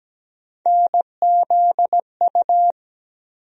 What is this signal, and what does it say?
Morse code, heard as a keyed tone, about 17 wpm, 700 Hz. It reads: NZU
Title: USA Call Sign Postfixes - 3 Letters 17wpm